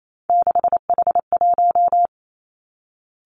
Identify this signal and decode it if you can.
Morse code, heard as a keyed tone, about 28 wpm, 700 Hz. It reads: 6H1
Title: Three Challenging Letter-Number Combinations 28wpm